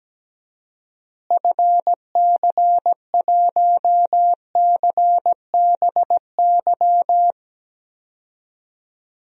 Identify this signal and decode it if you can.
Morse code, heard as a keyed tone, about 17 wpm, 700 Hz. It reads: FC1CBY
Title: DXCC-100 Call Signs - 6 Characters - Repeated 2x - 17wpm